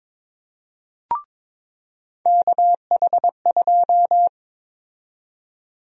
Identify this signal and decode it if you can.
Morse code, heard as a keyed tone, about 22 wpm, 700 Hz. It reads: KH2
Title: USA Call Sign Prefixes - 22wpm